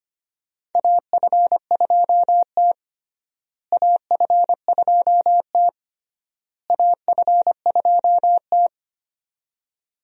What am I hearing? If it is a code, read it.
Morse code, heard as a keyed tone, about 25 wpm, 700 Hz. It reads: AF2T AF2T AF2T
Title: USA Call Signs - 4 Characters - Repeated 3x - 25wpm